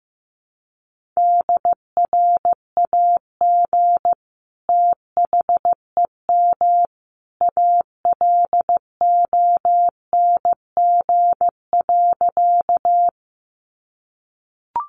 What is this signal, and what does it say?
Morse code, heard as a keyed tone, about 15 wpm, 700 Hz. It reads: DRAG THEM ALONG.
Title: Sentences from Top 1500 Words 15wpm